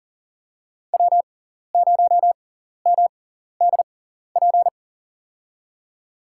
Morse code, heard as a keyed tone, about 40 wpm, spaced as for 14 wpm, 700 Hz. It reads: W0MDP
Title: USA Call Signs - 5 Characters - 6x Character Spacing - 40wpm